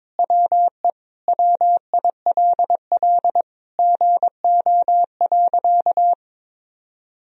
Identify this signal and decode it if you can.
Morse code, heard as a keyed tone, about 21 wpm, 700 Hz. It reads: WE WILL GO.
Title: Sentences from Top 1000 Words - Review Part 1 - 22wpm